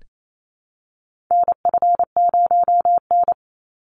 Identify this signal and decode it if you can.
Morse code, heard as a keyed tone, about 28 wpm, 700 Hz. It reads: NF0N
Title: Call Signs Repeated 3x - 28wpm